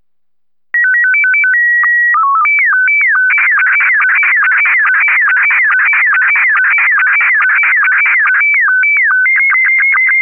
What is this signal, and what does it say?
SSTV (slow-scan television), Scottie 1.